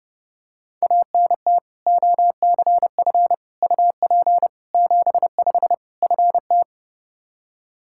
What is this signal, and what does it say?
Morse code, heard as a keyed tone, about 30 wpm, 700 Hz. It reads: ANT OCF UP 75 FT
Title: Practice QSO - Exchange 2 - Info - 30wpm